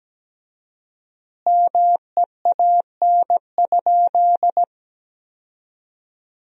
Morse code, heard as a keyed tone, about 17 wpm, 700 Hz. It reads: MEAN?
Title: 4x Word-Spacing - Sentences from Top 200 Words - 17wpm